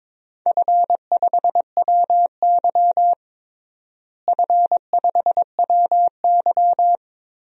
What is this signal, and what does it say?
Morse code, heard as a keyed tone, about 22 wpm, 700 Hz. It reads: F5WY F5WY
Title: DXCC-100 Call Signs - 4 Characters - Repeated 3x - 22wpm